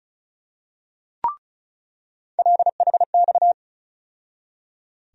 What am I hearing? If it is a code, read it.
Morse code, heard as a keyed tone, about 34 wpm, 700 Hz. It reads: LHX
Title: USA Call Sign Postfixes - 3 Letters 35wpm